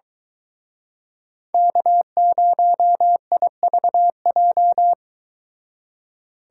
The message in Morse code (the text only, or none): K0IVJ